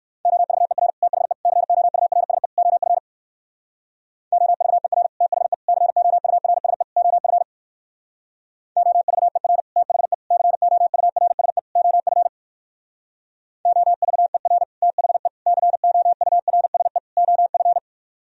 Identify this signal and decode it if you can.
Morse code, heard as a keyed tone, about 58 wpm, 700 Hz. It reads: OVER THE COURSE OF OVER THE COURSE OF OVER THE COURSE OF OVER THE COURSE OF